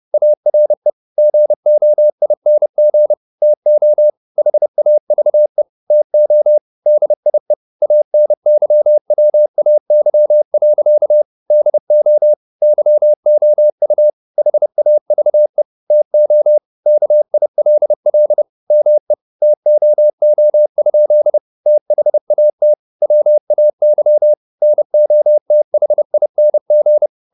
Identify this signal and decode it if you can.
Morse code, heard as a keyed tone, about 30 wpm, 600 Hz. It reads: ARE GOING TO HAVE TO DIE ANYWAY. DO YOU HAVE TO KILL ME TOO? THAT WAY NOTHING